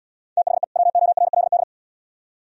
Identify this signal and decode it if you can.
Morse code, heard as a keyed tone, about 75 wpm, 700 Hz. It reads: THE WORLD